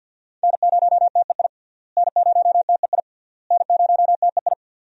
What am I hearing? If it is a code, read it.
Morse code, heard as a keyed tone, about 52 wpm, 700 Hz. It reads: N0TEI N0TEI N0TEI
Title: USA Call Signs - 5 Characters - Repeated 3x - 50wpm